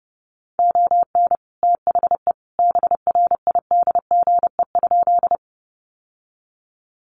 Morse code, heard as a keyed tone, about 30 wpm, 700 Hz. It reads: ON THE BRIDGE?
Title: Sentences from Top 1200 Words 30wpm